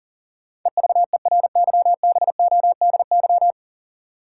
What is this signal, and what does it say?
Morse code, heard as a keyed tone, about 40 wpm, 700 Hz. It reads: EVERYBODY
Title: Sentences from Top 1400 Words - New Word Drill Alt - 40wpm